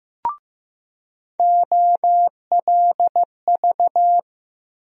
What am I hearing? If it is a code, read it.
Morse code, heard as a keyed tone, about 15 wpm, 700 Hz. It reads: OLV
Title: USA Call Sign Postfixes - 3 Letters 15wpm